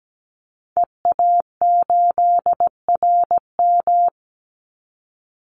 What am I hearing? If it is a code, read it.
Morse code, heard as a keyed tone, about 17 wpm, 700 Hz. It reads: EA8RM
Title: Call Signs - 17wpm